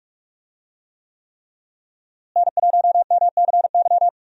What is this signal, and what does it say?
Morse code, heard as a keyed tone, about 43 wpm, 700 Hz. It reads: N1MCY
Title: USA Call Signs - 5 Characters - Repeated 3x - 45wpm